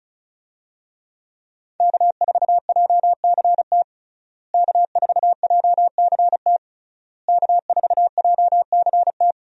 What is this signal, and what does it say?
Morse code, heard as a keyed tone, about 36 wpm, 700 Hz. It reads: K4JCT K4JCT K4JCT
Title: USA Call Signs - 5 Characters - Repeated 3x - 35wpm